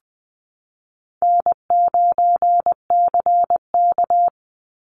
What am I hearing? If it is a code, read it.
Morse code, heard as a keyed tone, about 20 wpm, 700 Hz. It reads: N9CK
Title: Call Signs Repeated 3x - 20wpm